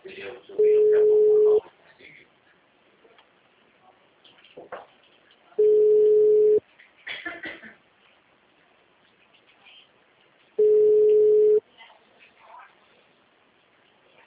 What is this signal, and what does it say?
Ringback tone (Europe, 425 Hz), with speech.